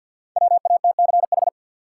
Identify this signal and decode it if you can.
Morse code, heard as a keyed tone, about 49 wpm, 700 Hz. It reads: WATCH
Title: Sentences from Top 300 Words - New Word Drill - 50wpm